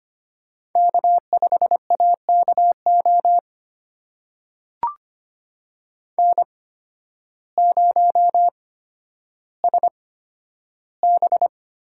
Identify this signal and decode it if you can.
Morse code, heard as a keyed tone, about 25 wpm, spaced as for 12 wpm, 700 Hz. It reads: K5AKO N0SB
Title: USA Call Signs - 8x Character Spacing - 25wpm